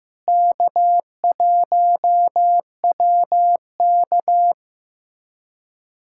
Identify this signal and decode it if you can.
Morse code, heard as a keyed tone, about 16 wpm, 700 Hz. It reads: K1WK
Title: USA Callsign DE Callsign - 4 Characters - 15wpm